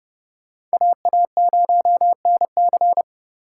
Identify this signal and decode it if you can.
Morse code, heard as a keyed tone, about 34 wpm, 700 Hz. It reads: AA0NC